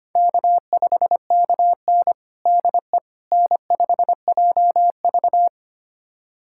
Morse code, heard as a keyed tone, about 25 wpm, 700 Hz. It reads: K5KN DE N5JV
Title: USA Callsign DE Callsign - 4 Characters - 25wpm